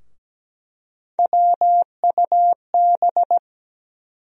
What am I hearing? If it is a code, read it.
Morse code, heard as a keyed tone, about 17 wpm, 700 Hz. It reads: WUB
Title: USA Call Sign Postfixes - 3 Letters 17wpm